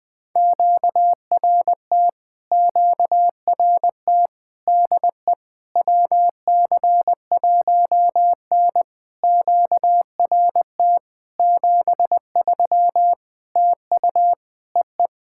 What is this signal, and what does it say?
Morse code, heard as a keyed tone, about 20 wpm, 700 Hz. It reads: QRT QRT DE WC1N QRT 73 TU EE
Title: POTA QSO Practice - 20wpm